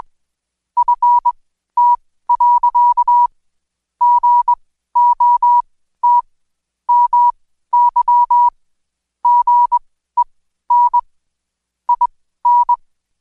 Morse code, heard as a keyed tone, about 21 wpm, spaced as for 13 wpm, 970 Hz. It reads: FT. GOT MY GEN IN
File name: Code Quick / AudioDownld2 / Seg124-Day29-DayDrill 29